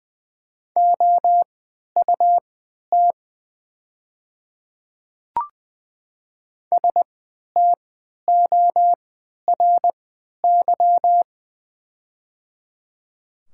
Morse code, heard as a keyed tone, about 20 wpm, spaced as for 11 wpm, 700 Hz. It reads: OUT STORY
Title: 3x Character Spacing - Top 300 Words - 20wpm